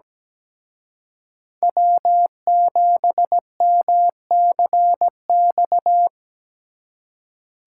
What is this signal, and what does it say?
Morse code, heard as a keyed tone, about 20 wpm, 700 Hz. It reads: W7MCX
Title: Speed Racing - USA 1x3 Call Signs 17wpm